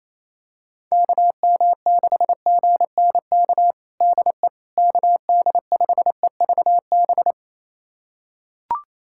Morse code, heard as a keyed tone, about 28 wpm, 700 Hz. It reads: KM6GNK DE KD5EVB
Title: USA Callsign DE Callsign - 6 Characters - 28wpm